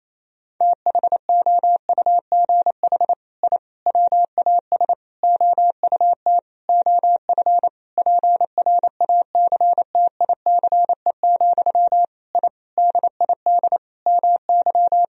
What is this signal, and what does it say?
Morse code, heard as a keyed tone, about 28 wpm, 700 Hz. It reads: THOUGH I WAS OUT OF PRACTICE, I DID MY
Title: Sentences from Top 1000 Words - Review Part 4 - 28wpm